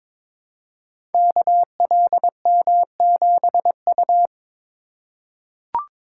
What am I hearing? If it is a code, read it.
Morse code, heard as a keyed tone, about 22 wpm, 700 Hz. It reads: KLM7U
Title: Code Groups - 5 Letter:Numbers - 22wpm